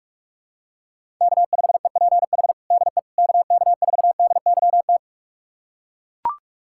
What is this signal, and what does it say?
Morse code, heard as a keyed tone, about 42 wpm, 700 Hz. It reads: K5EPH DE KK4DYT
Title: USA Callsign DE Callsign 45wpm